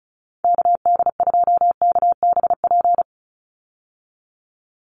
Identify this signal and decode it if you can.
Morse code, heard as a keyed tone, about 34 wpm, 700 Hz. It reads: KD2KBP
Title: USA Call Signs 35wpm